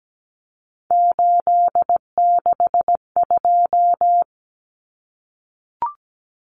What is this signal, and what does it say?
Morse code, heard as a keyed tone, about 17 wpm, 700 Hz. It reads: 862
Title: Common Cut Numbers - 3-Digit 17wpm